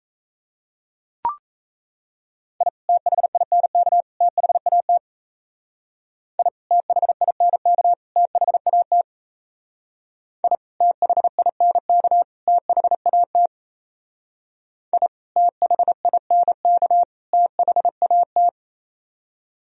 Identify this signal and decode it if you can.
Morse code, heard as a keyed tone, about 36 wpm, 700 Hz. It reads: I THINK THAT I THINK THAT I THINK THAT I THINK THAT